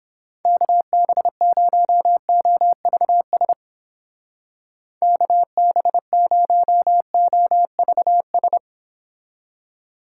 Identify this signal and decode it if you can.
Morse code, heard as a keyed tone, about 31 wpm, 700 Hz. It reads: KB0OVS KB0OVS